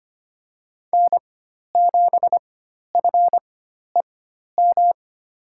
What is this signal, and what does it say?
Morse code, heard as a keyed tone, about 25 wpm, spaced as for 12 wpm, 700 Hz. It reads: N7FEM
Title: USA Call Signs - 5 Characters - 4x Character Spacing - 25wpm